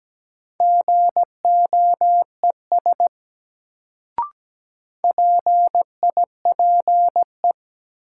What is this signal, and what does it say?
Morse code, heard as a keyed tone, about 17 wpm, 700 Hz. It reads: GOES PIPE
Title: Sentences from Top 1500 Words - New Word Drill Alt - 17wpm